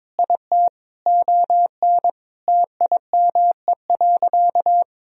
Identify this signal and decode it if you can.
Morse code, heard as a keyed tone, about 21 wpm, 700 Hz. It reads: IT ON TIME.
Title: Sentences from Top 1000 Words - Review Part 2 - 22wpm